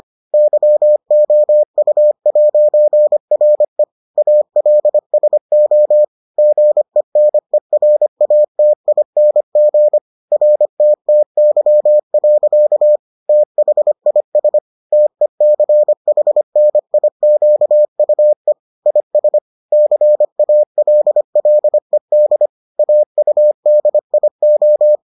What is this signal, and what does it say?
Morse code, heard as a keyed tone, about 25 wpm, 600 Hz. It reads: YOU'RE ALSO GENERATING RTTY. THIS TECHNIQUE IS CALLED AUDIO